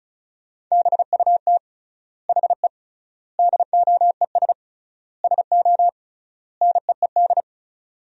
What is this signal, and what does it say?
Morse code, heard as a keyed tone, about 36 wpm, 700 Hz. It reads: BUT HE DOES SO NEED
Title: 3x Word-Spacing - Sentences from Top 200 Words - 35wpm